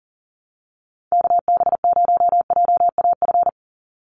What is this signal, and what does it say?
Morse code, heard as a keyed tone, about 40 wpm, 700 Hz. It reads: KB0JAF
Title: USA Call Signs 40wpm